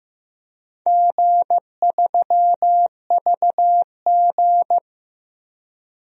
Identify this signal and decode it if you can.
Morse code, heard as a keyed tone, about 16 wpm, 700 Hz. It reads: G3VG
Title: DXCC-100 Call Signs - 4 Characters - Repeated 3x - 15wpm